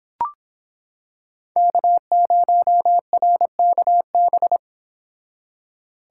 Morse code, heard as a keyed tone, about 25 wpm, 700 Hz. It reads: K0RKB